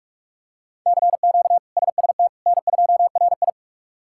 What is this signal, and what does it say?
Morse code, heard as a keyed tone, about 42 wpm, 700 Hz. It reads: CQ SST N2RI
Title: K1USN SST QSO Practice - 45wpm